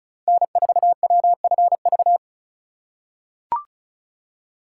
Morse code, heard as a keyed tone, about 36 wpm, 700 Hz. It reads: N4WFV
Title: USA Call Signs - 5 Characters - 2x Character Spacing - 35wpm